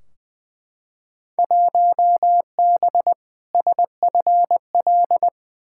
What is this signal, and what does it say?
Morse code, heard as a keyed tone, about 20 wpm, 700 Hz. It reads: 1B SFL
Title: Field Day Exchanges Encore - 20wpm